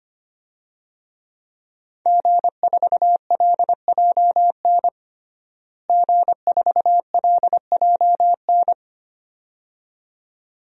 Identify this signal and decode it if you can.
Morse code, heard as a keyed tone, about 25 wpm, 700 Hz. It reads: G4LJN G4LJN
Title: DXCC-100 Call Signs - 5 Characters - Repeated 2x - 25wpm